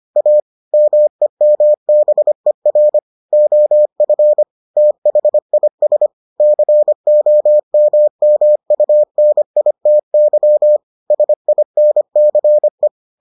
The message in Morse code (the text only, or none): A MEMBER OF THIS COMMUNITY SINCE